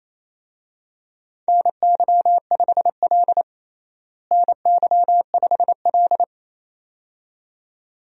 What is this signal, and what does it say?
Morse code, heard as a keyed tone, about 28 wpm, 700 Hz. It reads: NY5L NY5L
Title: USA Call Signs - 4 Characters - Repeated 2x - 28wpm